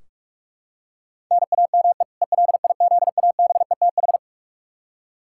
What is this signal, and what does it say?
Morse code, heard as a keyed tone, about 46 wpm, 700 Hz. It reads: NAME ELIZABETH
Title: K1USN SST QSO Practice 45wpm